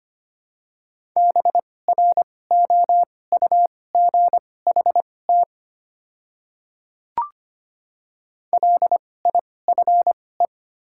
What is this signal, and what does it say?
Morse code, heard as a keyed tone, about 25 wpm, spaced as for 18 wpm, 700 Hz. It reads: BROUGHT LIFE
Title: 2x Character Spacing - Top 500 Words - 25wpm